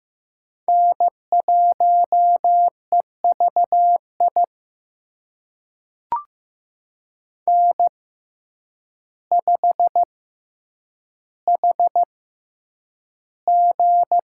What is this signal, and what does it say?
Morse code, heard as a keyed tone, about 15 wpm, spaced as for 8 wpm, 700 Hz. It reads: N1EVI N5HG
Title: USA Call Signs - 6x Character Spacing - 15wpm